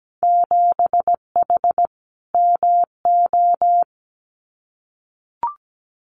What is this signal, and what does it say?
Morse code, heard as a keyed tone, about 17 wpm, 700 Hz. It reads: 7H MO
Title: Winter Field Day Exchanges Encore 17wpm